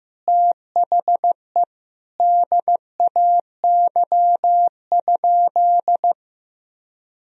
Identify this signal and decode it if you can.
Morse code, heard as a keyed tone, about 15 wpm, 700 Hz. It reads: THE DAY?
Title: Sentences from Top 1000 Words - Review Part 2 - 15wpm